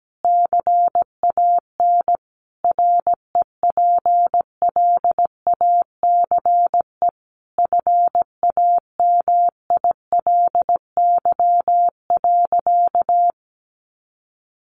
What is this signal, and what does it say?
Morse code, heard as a keyed tone, about 17 wpm, 700 Hz. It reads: CAN REPLACE FAMILY.
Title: Sentences from Top 1400 Words 17wpm